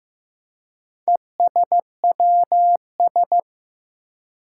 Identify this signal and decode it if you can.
Morse code, heard as a keyed tone, about 15 wpm, 700 Hz. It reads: ESWS
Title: Four Challenging Letter-Number Combinations 15wpm